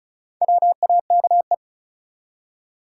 Morse code, heard as a keyed tone, about 34 wpm, 700 Hz. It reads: WAKE
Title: Four Letter Words - 35wpm